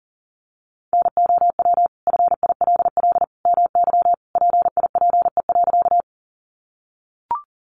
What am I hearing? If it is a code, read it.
Morse code, heard as a keyed tone, about 37 wpm, 700 Hz. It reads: NOW FILL MY PIPE.
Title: Sentences from Top 1500 Words 40wpm